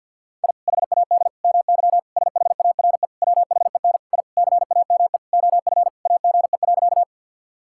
Morse code, heard as a keyed tone, about 50 wpm, 700 Hz. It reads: I HAD MY SHARE WHEN I CAME OF AGE.
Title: Sentences from Top 1000 Words - Review Part 6 - 50wpm